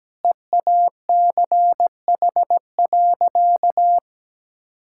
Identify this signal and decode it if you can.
Morse code, heard as a keyed tone, about 17 wpm, 700 Hz. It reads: EACH.
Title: Sentences from Top 1000 Words - Review Part 5 - 17wpm